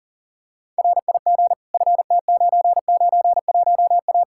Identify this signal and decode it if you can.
Morse code, heard as a keyed tone, about 40 wpm, 700 Hz. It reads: RIG FT991A
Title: Practice QSO - Exchange 2 - Info - 40wpm